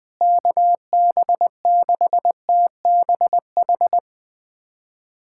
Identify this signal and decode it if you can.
Morse code, heard as a keyed tone, about 21 wpm, 700 Hz. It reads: KB6TBH
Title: USA Callsign DE Callsign - 6 Characters - 20wpm